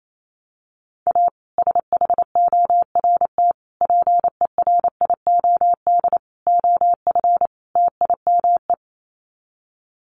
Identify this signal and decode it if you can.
Morse code, heard as a keyed tone, about 28 wpm, 700 Hz. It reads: A SHORT PERIOD OF TIME
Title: Sets of 5 Words Encore 28wpm